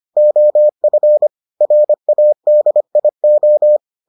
Morse code, heard as a keyed tone, about 25 wpm, 600 Hz. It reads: OF RADIO